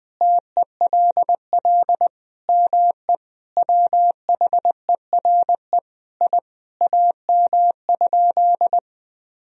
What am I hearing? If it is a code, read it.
Morse code, heard as a keyed tone, about 20 wpm, 700 Hz. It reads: TELL ME WHERE I AM?
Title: Sentences from Top 1000 Words - Review Part 2 - 20wpm